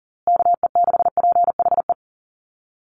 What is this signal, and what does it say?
Morse code, heard as a keyed tone, about 39 wpm, 700 Hz. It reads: KE6PHE
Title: USA Call Signs Repeated 3x 40wpm